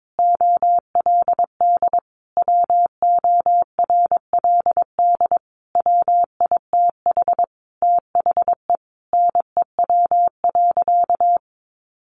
Morse code, heard as a keyed tone, about 22 wpm, 700 Hz. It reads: OLD WORLD WITH THE NEW.
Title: Sentences from Top 1300 Words 22wpm